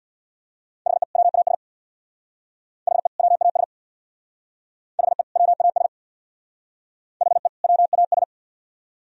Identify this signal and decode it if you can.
Morse code, heard as a keyed tone, about 62 wpm, 700 Hz. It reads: HE WAS HE WAS HE WAS HE WAS
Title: Speed-Racing - Sets of 2 Words 50wpm